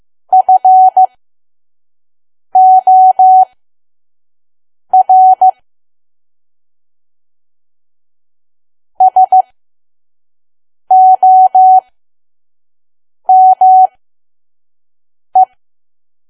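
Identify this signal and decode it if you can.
Morse code, heard as a keyed tone, about 15 wpm, spaced as for 5 wpm, 750 Hz. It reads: FOR SOME
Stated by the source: Morse code practice at 5 WPM from KQ2P.COM.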